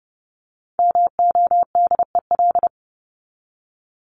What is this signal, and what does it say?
Morse code, heard as a keyed tone, about 30 wpm, 700 Hz. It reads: MODEL
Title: Top 501-750 Words 30wpm